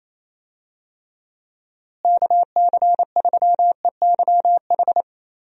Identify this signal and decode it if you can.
Morse code, heard as a keyed tone, about 27 wpm, 700 Hz. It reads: KC3EYH
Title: USA Call Signs - 6 Characters - Repeated 2x - 28wpm